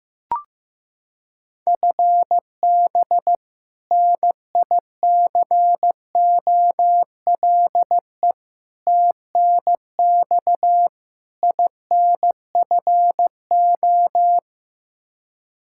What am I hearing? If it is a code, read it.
Morse code, heard as a keyed tone, about 15 wpm, 700 Hz. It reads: FB NICOLE TNX INFO
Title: Practice QSO - Exchange 3 - More Info - 15wpm